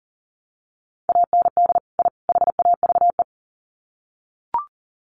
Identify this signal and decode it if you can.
Morse code, heard as a keyed tone, about 40 wpm, 700 Hz. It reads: AND I HAVE
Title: Sets of 3 Words 40wpm